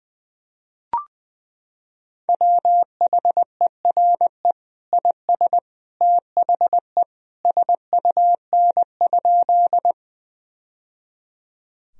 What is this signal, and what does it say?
Morse code, heard as a keyed tone, about 21 wpm, 700 Hz. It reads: WHERE IS THE SUN?
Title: Sentences from Top 1000 Words - Review Part 1 - 20wpm